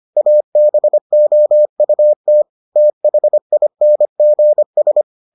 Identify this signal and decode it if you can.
Morse code, heard as a keyed tone, about 25 wpm, 600 Hz. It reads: ABOUT THINGS